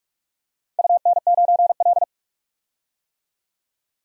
Morse code, heard as a keyed tone, about 44 wpm, 700 Hz. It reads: UN9L
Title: DXCC-100 Call Signs - 4 Characters - 45wpm